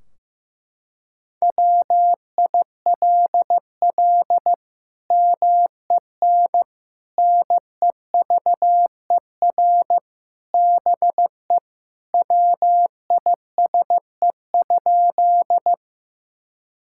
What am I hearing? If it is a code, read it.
Morse code, heard as a keyed tone, about 15 wpm, 700 Hz. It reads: WILL MEN NEVER BE WISE?
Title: Sentences from Top 1600 Words 15wpm